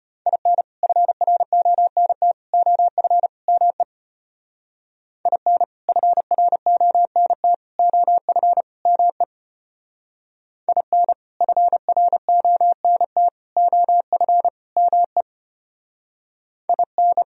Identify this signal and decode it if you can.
Morse code, heard as a keyed tone, about 30 wpm, 700 Hz. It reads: IN FRONT OF ME IN FRONT OF ME IN FRONT OF ME IN
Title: Speed-Racing - Sets of 4 Words 25wpm